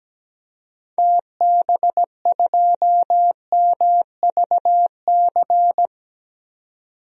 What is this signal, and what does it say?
Morse code, heard as a keyed tone, about 17 wpm, 700 Hz. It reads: TB2MVC
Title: DXCC-100 Call Signs - 6 Characters - 17wpm